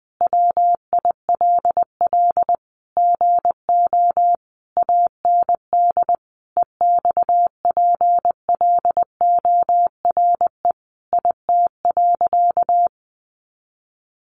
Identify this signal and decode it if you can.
Morse code, heard as a keyed tone, about 20 wpm, 700 Hz. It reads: WILL GO AND EXPLORE IT.
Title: Sentences from Top 1100 Words 20wpm